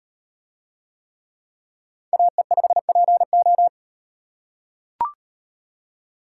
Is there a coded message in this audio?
Yes: Morse code from 2.1 to 3.7 s, heard as a keyed tone, about 38 wpm, 700 Hz. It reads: AE5PO